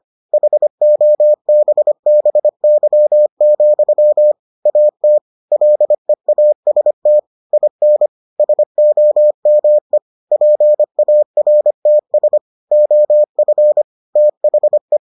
Morse code, heard as a keyed tone, about 25 wpm, 600 Hz. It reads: HOBBY, AT LEAST IN SOME PARTS OF THE